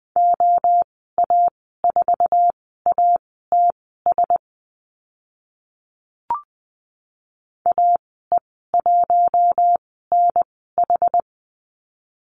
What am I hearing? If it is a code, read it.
Morse code, heard as a keyed tone, about 20 wpm, spaced as for 15 wpm, 700 Hz. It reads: OA4ATS AE1NH
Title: 062 Lesson 11g - Letter h - Call Signs 20wpm